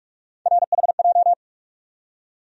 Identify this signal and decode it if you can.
Morse code, heard as a keyed tone, about 47 wpm, 700 Hz. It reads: RHJ
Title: USA Call Sign Postfixes - 3 Letters 45wpm